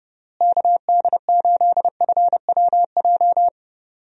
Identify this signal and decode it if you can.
Morse code, heard as a keyed tone, about 30 wpm, 700 Hz. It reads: KD8FWJ
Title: USA Call Signs - 6 Characters - Repeated 3x - 30wpm